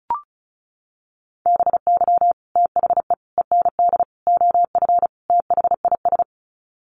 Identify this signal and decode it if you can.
Morse code, heard as a keyed tone, about 34 wpm, 700 Hz. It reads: BY THE END OF THIS
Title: Sets of 5 Words Encore 35wpm